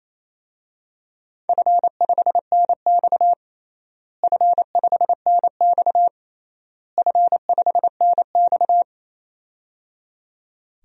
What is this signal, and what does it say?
Morse code, heard as a keyed tone, about 28 wpm, 700 Hz. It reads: F5NX F5NX F5NX
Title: DXCC-100 Call Signs - 4 Characters - Repeated 3x - 28wpm